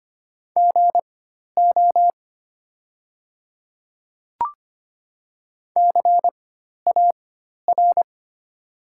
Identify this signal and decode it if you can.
Morse code, heard as a keyed tone, about 25 wpm, spaced as for 12 wpm, 690 Hz. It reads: GO CAR
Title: 4x Character Spacing - Top 300 Words - 25wpm